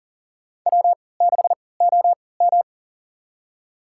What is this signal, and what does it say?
Morse code, heard as a keyed tone, about 40 wpm, spaced as for 23 wpm, 700 Hz. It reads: W6OM
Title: USA Call Signs - 4 Characters - 3x Character Spacing - 40wpm